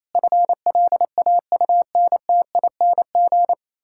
Morse code, heard as a keyed tone, about 28 wpm, 700 Hz. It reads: FLAUNTING
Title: Pangram Challenge v2 - 28wpm